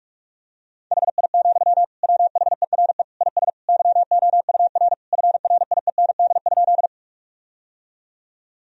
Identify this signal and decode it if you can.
Morse code, heard as a keyed tone, about 46 wpm, 700 Hz. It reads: HI, WHERE IS YOUR FRIEND?
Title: Sentences from Top 1000 Words - Review Part 3 - 45wpm